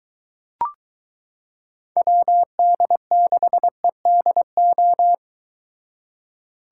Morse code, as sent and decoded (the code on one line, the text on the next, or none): .-- -.. -.... -.. ---
WD6EDO